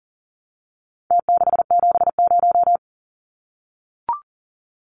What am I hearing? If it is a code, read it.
Morse code, heard as a keyed tone, about 38 wpm, 700 Hz. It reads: T670
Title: Numbers - Contest Style - 4-Digit 40wpm